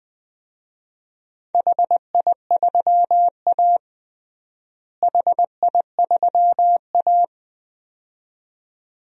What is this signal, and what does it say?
Morse code, heard as a keyed tone, about 21 wpm, 700 Hz. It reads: HI3A HI3A
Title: DXCC-100 Call Signs - 4 Characters - Repeated 2x - 20wpm